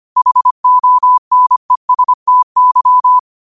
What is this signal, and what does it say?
Morse code, heard as a keyed tone, about 25 wpm, 1000 Hz. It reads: HONESTY